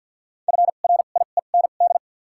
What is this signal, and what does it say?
Morse code, heard as a keyed tone, about 48 wpm, spaced as for 35 wpm, 700 Hz. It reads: FRIEND
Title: 2x Character Spacing - Top 500 Words - 50wpm